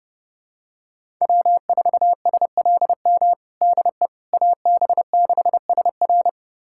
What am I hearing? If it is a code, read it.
Morse code, heard as a keyed tone, about 30 wpm, 700 Hz. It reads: W4SLM DE AB6SR
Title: USA Callsign DE Callsign - 5 Characters - 30wpm